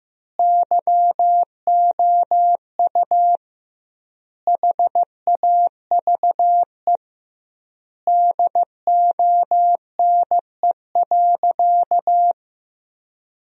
Morse code, heard as a keyed tone, about 15 wpm, 700 Hz. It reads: YOU HAVE DONE.
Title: 2x Word-Spacing - Sentences from Top 500 Words - 15wpm